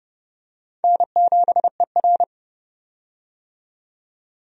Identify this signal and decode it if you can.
Morse code, heard as a keyed tone, about 32 wpm, 700 Hz. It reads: N7ER
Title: USA Call Signs - 4 Characters - 30wpm